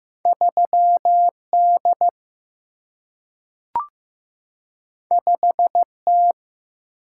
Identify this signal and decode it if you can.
Morse code, heard as a keyed tone, about 15 wpm, 700 Hz. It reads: 3D 5T
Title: Single-Letter Single-Number 15wpm